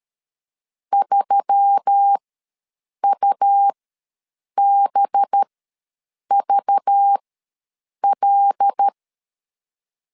Morse code, heard as a keyed tone, about 13 wpm, spaced as for 7 wpm, 800 Hz. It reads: 3UBVL
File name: LEZIONI / 5 LBVC / 5 LBV ws 12wpm 3015